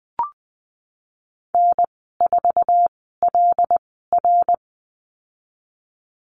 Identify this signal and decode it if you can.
Morse code, heard as a keyed tone, about 20 wpm, spaced as for 14 wpm, 700 Hz. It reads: N4LR
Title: 122 Lesson 20g - Number 3 - Call Signs 20wpm